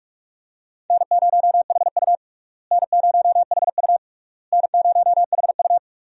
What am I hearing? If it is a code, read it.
Morse code, heard as a keyed tone, about 46 wpm, 700 Hz. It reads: N0HU N0HU N0HU
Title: USA Call Signs - 4 Characters - Repeated 3x - 45wpm